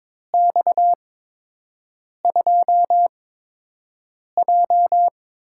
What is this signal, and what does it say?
Morse code, heard as a keyed tone, about 21 wpm, spaced as for 6 wpm, 700 Hz. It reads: X2J
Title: USA Call Signs - 4 Characters - 8x Character Spacing - 22wpm